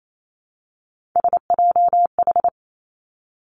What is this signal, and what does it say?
Morse code, heard as a keyed tone, about 28 wpm, 700 Hz. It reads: SJH
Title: Three Challenging Letter-Number Combinations 28wpm